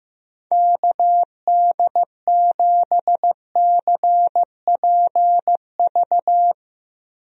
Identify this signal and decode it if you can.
Morse code, heard as a keyed tone, about 16 wpm, 700 Hz. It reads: KD7CPV
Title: USA Call Signs - 6 Characters - Repeated 2x - 15wpm